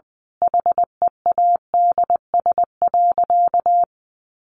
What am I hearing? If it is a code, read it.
Morse code, heard as a keyed tone, about 20 wpm, 700 Hz. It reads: HEADS.
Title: A Sound Of Thunder 20wpm